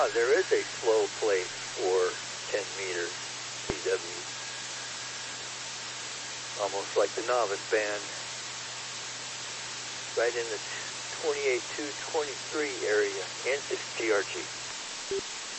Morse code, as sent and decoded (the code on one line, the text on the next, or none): none